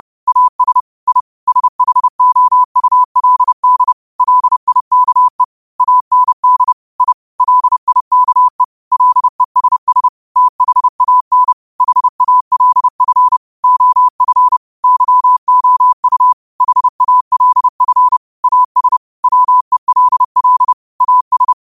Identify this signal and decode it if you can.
Morse code, heard as a keyed tone, about 30 wpm, 1000 Hz. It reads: AS I SHOULD LIKE AND I LIKE LESS THAN HALF OF YOU HALF AS WELL AS
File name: qotdfiles / 30wpm